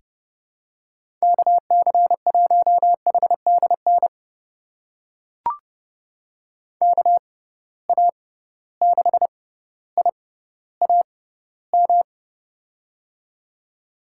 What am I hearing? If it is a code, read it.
Morse code, heard as a keyed tone, about 31 wpm, spaced as for 16 wpm, 700 Hz. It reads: KC1HDN KA6IAM
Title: USA Call Signs - 6 Characters - 6x Character Spacing - 30wpm